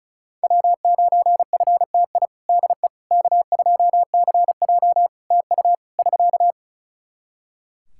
Morse code, heard as a keyed tone, about 34 wpm, 700 Hz. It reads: W9FTI DE K2CJ TU <SK>
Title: K1USN SST QSO Practice 35wpm